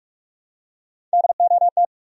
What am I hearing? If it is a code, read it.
Morse code, heard as a keyed tone, about 46 wpm, 700 Hz. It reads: DOT
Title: USA Common Names - Max 3 Letters - 45wpm